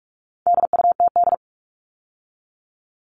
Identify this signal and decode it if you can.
Morse code, heard as a keyed tone, about 44 wpm, 700 Hz. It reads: DUTD
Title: All Cut Numbers - 4-Digit 45wpm